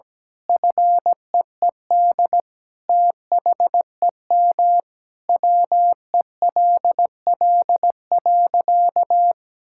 Morse code, heard as a keyed tone, about 17 wpm, 700 Hz. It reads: FEED THEM WELL.
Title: Sentences from Top 1600 Words 17wpm